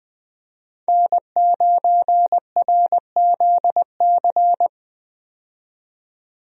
Morse code, heard as a keyed tone, about 20 wpm, 700 Hz. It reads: N9RZC
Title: USA Call Signs - 5 Characters - 20wpm